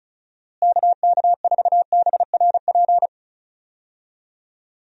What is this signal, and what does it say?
Morse code, heard as a keyed tone, about 34 wpm, 700 Hz. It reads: KK4BRP
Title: USA Call Signs - 6 Characters - Repeated 2x - 35wpm